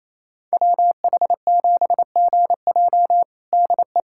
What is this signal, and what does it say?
Morse code, heard as a keyed tone, about 28 wpm, 700 Hz. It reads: WH7GJ DE
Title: USA Callsign DE Callsign 28wpm